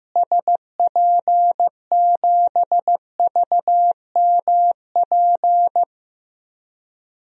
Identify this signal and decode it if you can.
Morse code, heard as a keyed tone, about 15 wpm, 700 Hz. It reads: SP7VMP
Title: DXCC-100 Call Signs - 6 Characters - Repeated 3x - 15wpm